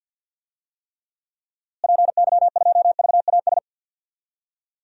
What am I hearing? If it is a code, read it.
Morse code, heard as a keyed tone, about 47 wpm, 700 Hz. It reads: PY2VAS
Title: DXCC-100 Call Signs - 6 Characters - Repeated 2x - 50wpm